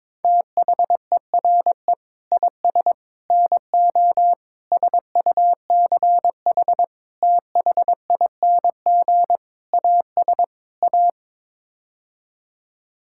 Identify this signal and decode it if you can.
Morse code, heard as a keyed tone, about 22 wpm, 700 Hz. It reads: THERE IS NO SUCH THING AS A
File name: practice / 22wpm / Sets of 7 Words - 22wpm